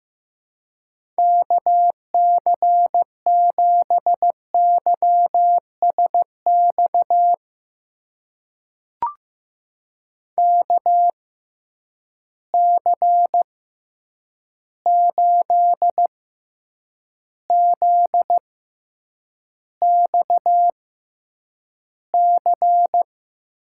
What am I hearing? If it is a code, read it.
Morse code, heard as a keyed tone, about 15 wpm, spaced as for 8 wpm, 700 Hz. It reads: KC7YSX KC8ZXC
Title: USA Call Signs - 6 Characters - 6x Character Spacing - 15wpm